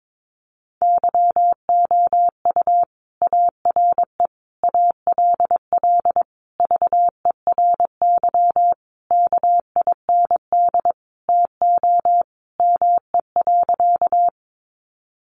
Morse code, heard as a keyed tone, about 22 wpm, 700 Hz. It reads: YOU ARE ALL VERY KIND TO ME.
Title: Sentences from Top 200 Words 22wpm